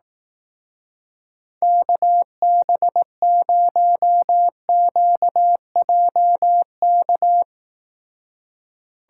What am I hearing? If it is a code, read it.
Morse code, heard as a keyed tone, about 18 wpm, 700 Hz. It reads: KB0QJK
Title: Speed Racing - USA 2x3 Call Signs 15wpm